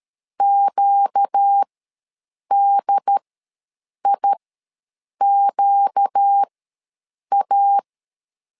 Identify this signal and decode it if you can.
Morse code, heard as a keyed tone, about 13 wpm, spaced as for 7 wpm, 800 Hz. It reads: QDIQA
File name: LEZIONI / 4 Q ws 12wpm 1912